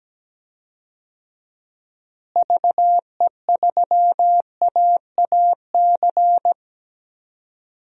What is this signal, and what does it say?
Morse code, heard as a keyed tone, about 17 wpm, 700 Hz. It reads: VE3AAC
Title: DXCC-100 Call Signs - 6 Characters - Repeated 3x - 17wpm